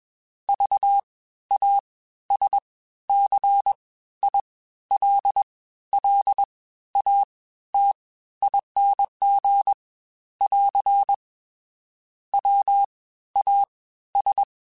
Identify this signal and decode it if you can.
Morse code, heard as a keyed tone, about 21 wpm, spaced as for 13 wpm, 800 Hz. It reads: VASCILLATING <AR> WAS